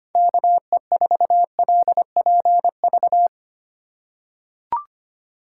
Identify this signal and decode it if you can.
Morse code, heard as a keyed tone, about 25 wpm, 700 Hz. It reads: KE4LPV
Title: USA Call Signs - 4x Character Spacing - 25wpm